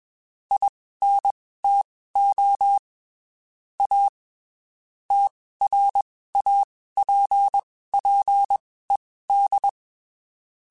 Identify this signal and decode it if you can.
Morse code, heard as a keyed tone, about 21 wpm, spaced as for 15 wpm, 800 Hz. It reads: INTO A TRAPPED